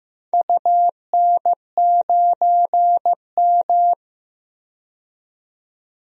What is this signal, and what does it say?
Morse code, heard as a keyed tone, about 15 wpm, 700 Hz. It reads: UN9M
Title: DXCC-100 Call Signs - 4 Characters - 15wpm